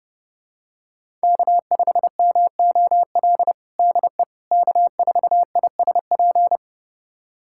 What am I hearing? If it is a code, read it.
Morse code, heard as a keyed tone, about 30 wpm, 700 Hz. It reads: K5MOL DE K4ISP
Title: USA Callsign DE Callsign - 5 Characters - 30wpm